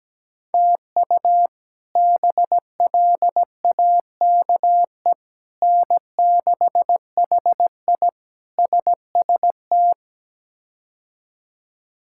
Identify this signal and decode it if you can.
Morse code, heard as a keyed tone, about 17 wpm, 700 Hz. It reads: TU BLAKE N6HI SST
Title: K1USN SST QSO Practice - 17wpm